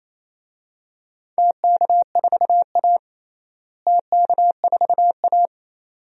Morse code, heard as a keyed tone, about 28 wpm, 700 Hz. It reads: TK4A TK4A
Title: DXCC-100 Call Signs - 4 Characters - Repeated 2x - 28wpm